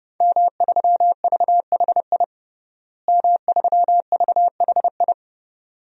Morse code, heard as a keyed tone, about 32 wpm, 700 Hz. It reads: M3VHI M3VHI
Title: DXCC-100 Call Signs - 5 Characters - Repeated 2x - 30wpm